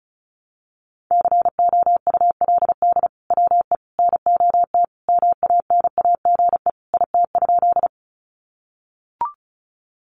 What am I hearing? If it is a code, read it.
Morse code, heard as a keyed tone, about 34 wpm, 700 Hz. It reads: COULD WE NOT MANAGE IT?
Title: Sentences from Top 1400 Words 35wpm